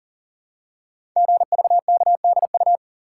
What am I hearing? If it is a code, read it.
Morse code, heard as a keyed tone, about 40 wpm, 700 Hz. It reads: GVKDU
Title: Code Groups - 5 Letter:Numbers - 40wpm